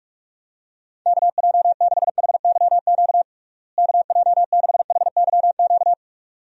Morse code, heard as a keyed tone, about 46 wpm, 700 Hz. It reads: KJ6HYQ KJ6HYQ
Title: USA Call Signs - 6 Characters - Repeated 2x - 45wpm